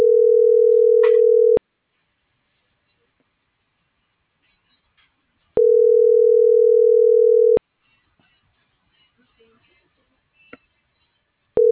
Ringback tone (North America), 440 + 480 Hz.